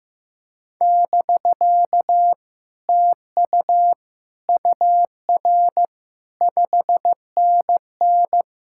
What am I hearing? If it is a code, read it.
Morse code, heard as a keyed tone, about 15 wpm, 700 Hz. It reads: <BK> TU UR 5NN
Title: Speed-Racing Common QSO Phrases Encore - 15wpm